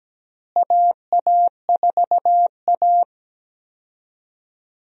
Morse code, heard as a keyed tone, about 17 wpm, 700 Hz. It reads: AA4A
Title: USA Call Signs - 4 Characters - Repeated 2x - 17wpm